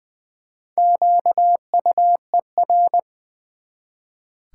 Morse code, heard as a keyed tone, about 20 wpm, 700 Hz. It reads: QUER
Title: Build A Word - 1 Letter At A Time - 5-Letter Words - 20wpm